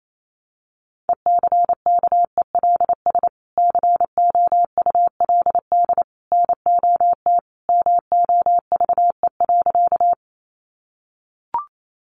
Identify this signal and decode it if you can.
Morse code, heard as a keyed tone, about 28 wpm, 700 Hz. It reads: ECKELS COULD NOT MOVE.
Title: A Sound Of Thunder 28wpm